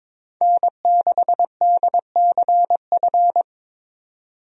Morse code, heard as a keyed tone, about 22 wpm, 700 Hz. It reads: N6DCF
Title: USA Call Signs - 5 Characters - 22wpm